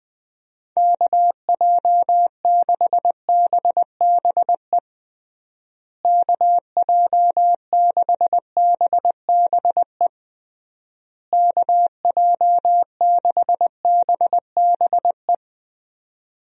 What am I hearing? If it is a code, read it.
Morse code, heard as a keyed tone, about 21 wpm, 700 Hz. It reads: KJ6BBE KJ6BBE KJ6BBE
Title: USA Call Signs - 6 Characters - Repeated 3x - 20wpm